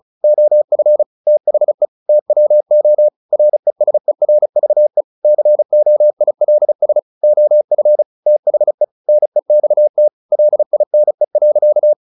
Morse code, heard as a keyed tone, about 34 wpm, 600 Hz. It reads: OF THE TWO RESERVE COILS OF THE NEXT LINE.
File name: the old man and the sea / the old man and the sea 1 35wpm